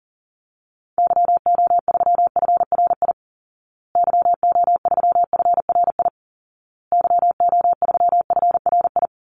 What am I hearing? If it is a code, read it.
Morse code, heard as a keyed tone, about 40 wpm, 700 Hz. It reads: YO3FRI YO3FRI YO3FRI
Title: Call Signs Repeated 2x - 40wpm